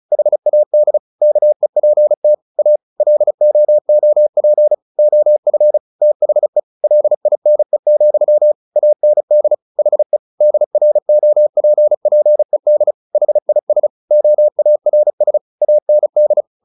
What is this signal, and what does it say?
Morse code, heard as a keyed tone, about 34 wpm, 600 Hz. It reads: HAD KEPT A LOOP OF THE LINE, AND HE DROPPED HIS OARS AND
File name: the old man and the sea / the old man and the sea 1 35wpm